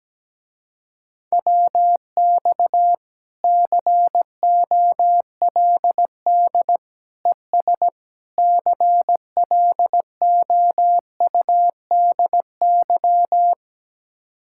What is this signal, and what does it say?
Morse code, heard as a keyed tone, about 17 wpm, 700 Hz. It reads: WX COLD ES CLOUDY
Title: Speed-Racing Common QSO Phrases - 17wpm